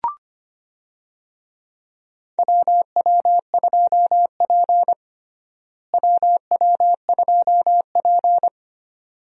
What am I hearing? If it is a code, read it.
Morse code, heard as a keyed tone, about 25 wpm, 700 Hz. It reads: WW2P WW2P
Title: USA Call Signs - 4 Characters - Repeated 2x - 25wpm